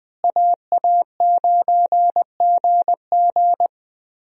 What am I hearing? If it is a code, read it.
Morse code, heard as a keyed tone, about 21 wpm, 700 Hz. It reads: AA9GG
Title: USA Call Signs - 5 Characters - Repeated 2x - 20wpm